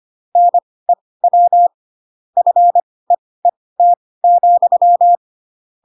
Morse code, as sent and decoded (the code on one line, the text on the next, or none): -. .-- / ..-. . - --..--
NEW FEET,